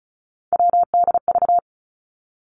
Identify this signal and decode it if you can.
Morse code, heard as a keyed tone, about 36 wpm, 700 Hz. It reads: WDV
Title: Three Challenging Letter-Number Combinations 35wpm